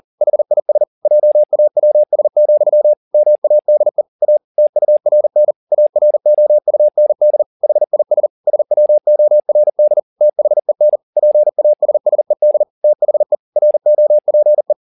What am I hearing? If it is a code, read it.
Morse code, heard as a keyed tone, about 40 wpm, 600 Hz. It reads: HIS JAWS, MADE A TURN AROUND HIS SWORD THEN PASSED THE ROPE
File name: the old man and the sea 2 40wpm